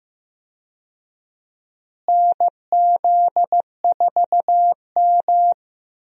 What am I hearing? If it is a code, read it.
Morse code, heard as a keyed tone, about 15 wpm, 700 Hz. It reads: NZ4M
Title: USA Call Signs - 4 Characters - Repeated 2x - 15wpm